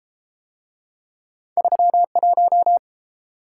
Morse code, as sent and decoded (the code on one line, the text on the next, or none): ...-- .----
31